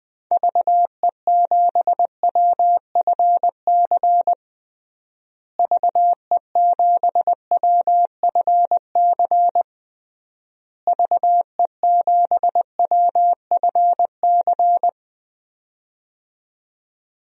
Morse code, heard as a keyed tone, about 21 wpm, 700 Hz. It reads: VE7WFC VE7WFC VE7WFC
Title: DXCC-100 Call Signs - 6 Characters - Repeated 3x - 20wpm